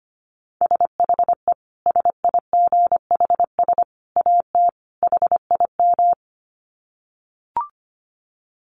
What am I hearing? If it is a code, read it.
Morse code, heard as a keyed tone, about 25 wpm, 700 Hz. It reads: SHE SIGHS AT HIM
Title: Dit Challenge 25wpm